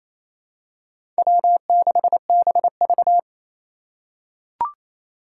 Morse code, heard as a keyed tone, about 27 wpm, 700 Hz. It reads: W6BV
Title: Four Challenging Letter-Number Combinations 28wpm